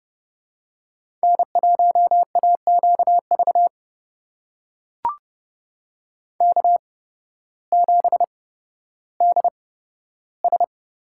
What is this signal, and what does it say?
Morse code, heard as a keyed tone, about 30 wpm, spaced as for 14 wpm, 700 Hz. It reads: N1AQV K7DS
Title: USA Call Signs - 5 Characters - 8x Character Spacing - 30wpm